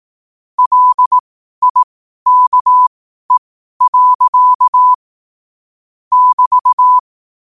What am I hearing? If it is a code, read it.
Morse code, heard as a keyed tone, about 18 wpm, spaced as for 12 wpm, 1000 Hz. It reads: LIKE. <BT>
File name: qotdfiles / 12wpm